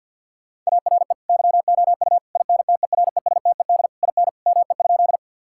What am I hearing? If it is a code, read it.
Morse code, heard as a keyed tone, about 50 wpm, 700 Hz. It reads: ARE YOU INTERESTED IN ME?
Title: Sentences from Top 1000 Words - Review Part 6 - 50wpm